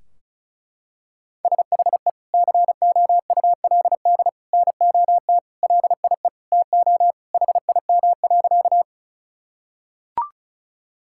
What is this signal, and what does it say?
Morse code, heard as a keyed tone, about 36 wpm, 700 Hz. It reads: SHE COULD NOT LIE TO HIM.
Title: Sentences from Top 1600 Words 35wpm